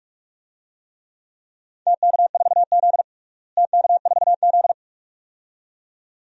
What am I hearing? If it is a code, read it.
Morse code, heard as a keyed tone, about 44 wpm, 700 Hz. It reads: TK4Z TK4Z
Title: DXCC-100 Call Signs - 4 Characters - Repeated 2x - 45wpm